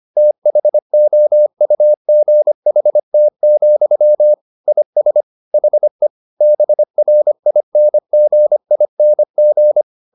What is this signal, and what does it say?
Morse code, heard as a keyed tone, about 25 wpm, 600 Hz. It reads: THOUGHT, IS HE BRINGING